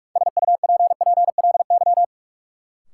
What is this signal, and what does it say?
Morse code, heard as a keyed tone, about 46 wpm, 700 Hz. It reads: SUPPLY
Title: Sentences from Top 1000 Words - Review Part 4 - New Word Drill - 45wpm